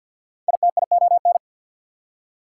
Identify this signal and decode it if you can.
Morse code, heard as a keyed tone, about 69 wpm, spaced as for 59 wpm, 690 Hz. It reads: ITION
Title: Speed-Racing Prefixes and Suffixes 50wpm